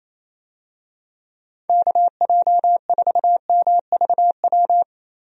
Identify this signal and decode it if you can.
Morse code, heard as a keyed tone, about 27 wpm, 700 Hz. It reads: KJ4MVW
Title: USA Call Signs - 6 Characters - Repeated 3x - 28wpm